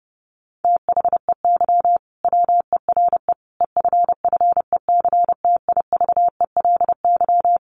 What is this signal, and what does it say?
Morse code, heard as a keyed tone, about 28 wpm, 700 Hz. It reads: THEY WERE EFFECTIVELY
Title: Sentences from Top 1400 Words 30wpm